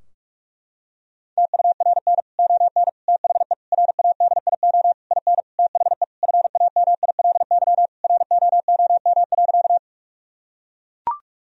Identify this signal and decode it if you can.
Morse code, heard as a keyed tone, about 41 wpm, 700 Hz. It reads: TURN ON THE RADIO IN THE FAMILY ROOM.
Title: Sentences from Top 1000 Words - Review Part 3 - 45wpm